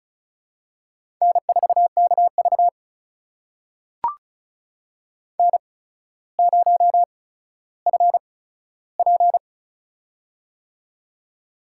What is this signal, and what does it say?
Morse code, heard as a keyed tone, about 36 wpm, spaced as for 15 wpm, 700 Hz. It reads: N4KV N0FP
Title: USA Call Signs - 4 Characters - 8x Character Spacing - 35wpm